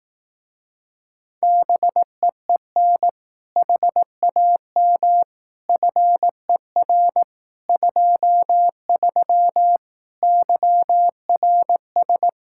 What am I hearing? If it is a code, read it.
Morse code, heard as a keyed tone, about 18 wpm, 700 Hz. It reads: BEEN HAM FER 23 YRS
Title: Speed-Racing Common QSO Phrases Encore - 15wpm